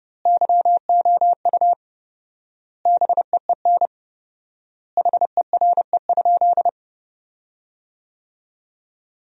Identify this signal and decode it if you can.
Morse code, heard as a keyed tone, about 30 wpm, 700 Hz. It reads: YOU BEEN HERE?
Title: 4x Word-Spacing - Sentences from Top 200 Words - 30wpm